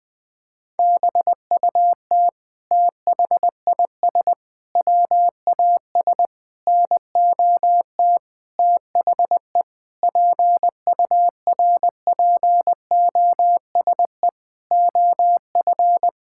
Morse code, heard as a keyed tone, about 19 wpm, 700 Hz. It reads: BUT THIS WAS NOT THE PURPOSE OF
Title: Sentences from Top 1000 Words - Review Part 6 - 20wpm